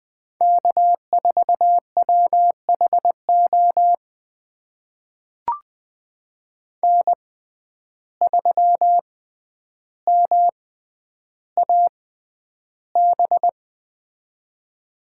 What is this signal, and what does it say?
Morse code, heard as a keyed tone, about 20 wpm, spaced as for 10 wpm, 700 Hz. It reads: K4WHO N3MAB
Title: USA Call Signs - 5 Characters - 6x Character Spacing - 20wpm